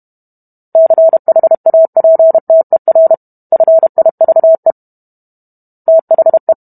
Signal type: Morse code